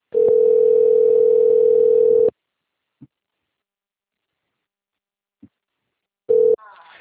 Ringback tone (North America), 440 + 480 Hz.